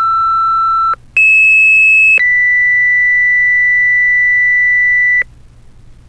Two-tone page, heard as a one-second tone, then a three-second tone, about 2600 then 2000 Hz, with noise. It is heard from 1.2 to 5.2 s.
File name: fire-pager